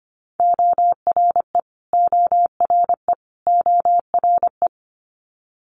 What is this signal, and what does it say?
Morse code, heard as a keyed tone, about 25 wpm, 700 Hz. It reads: ORE ORE ORE
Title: Multi-letter Phonemes Repeated 3x 25wpm